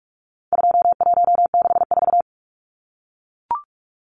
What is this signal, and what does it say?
Morse code, heard as a keyed tone, about 44 wpm, 700 Hz. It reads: 2164
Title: Numbers - 4-Digit 45wpm